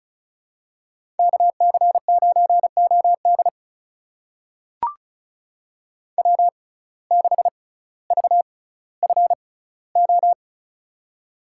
Morse code, heard as a keyed tone, about 34 wpm, spaced as for 18 wpm, 700 Hz. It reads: KC9OD W6VFO
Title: USA Call Signs - 5 Characters - 6x Character Spacing - 35wpm